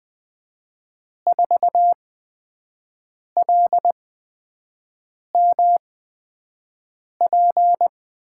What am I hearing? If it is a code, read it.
Morse code, heard as a keyed tone, about 20 wpm, spaced as for 5 wpm, 700 Hz. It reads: 4LMP
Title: USA Call Signs - 6 Characters - 8x Character Spacing - 20wpm